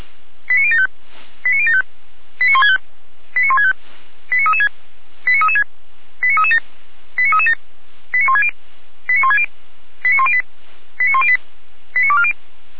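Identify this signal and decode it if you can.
Five-tone selective call (ZVEI), heard as quick runs of five tones, tones of about 70 ms. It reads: 89075 89075 89165 89165 89336 89336 89337 89337 89160 89160 89108 89108 89240
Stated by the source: Five-tone ZVEI selective call received on SDR.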